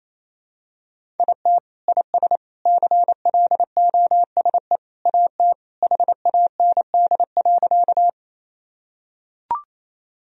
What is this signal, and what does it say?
Morse code, heard as a keyed tone, about 28 wpm, 700 Hz. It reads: IT IS CLOSE AT HAND.
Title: Sentences from Top 1000 Words - Review Part 1 - 28wpm